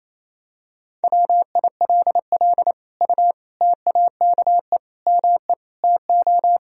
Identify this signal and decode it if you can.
Morse code, heard as a keyed tone, about 28 wpm, 700 Hz. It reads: WILL U TAKE ME TO
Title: Sentences from Top 200 Words and Abbreviations 28wpm